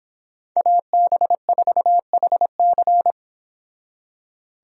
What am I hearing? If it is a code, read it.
Morse code, heard as a keyed tone, about 29 wpm, 700 Hz. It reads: AB4HC